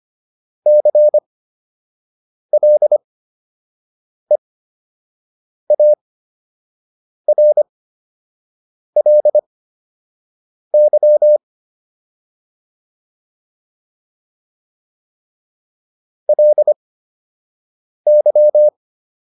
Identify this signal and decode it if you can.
Morse code, heard as a keyed tone, about 25 wpm, spaced as for 6 wpm, 600 Hz. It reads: CLEARLY LY